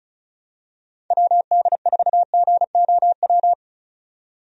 Morse code, heard as a keyed tone, about 36 wpm, 700 Hz. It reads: WD4GOW
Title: USA Call Signs - 6 Characters - 35wpm